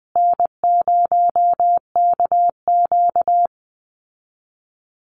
Morse code, heard as a keyed tone, about 20 wpm, 700 Hz. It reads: N0KQ
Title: Easy Call Signs Repeated 3x - 20wpm